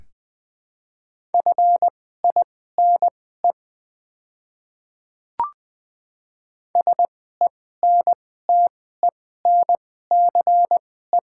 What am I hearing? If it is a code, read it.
Morse code, heard as a keyed tone, about 20 wpm, spaced as for 14 wpm, 700 Hz. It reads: FINE SENTENCE
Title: 2x Character Spacing - Top 500 Words - 20wpm